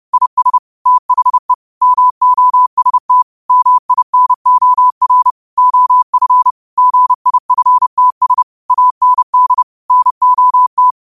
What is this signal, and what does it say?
Morse code, heard as a keyed tone, about 30 wpm, 1000 Hz. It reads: IS THE MOST MINOR OF GIFTS AND NOT